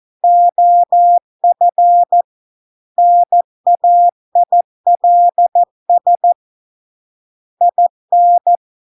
Morse code, heard as a keyed tone, about 14 wpm, 700 Hz. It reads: OF NAILS IN